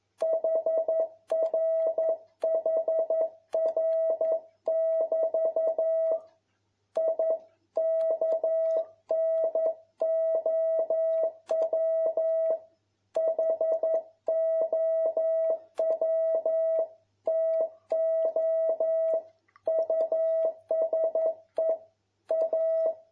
Morse code, heard as a keyed tone, about 11 wpm, 650 Hz. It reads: HRHR<BT> I KNOW HOW TO USE A
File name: jh8cbh / 2021cw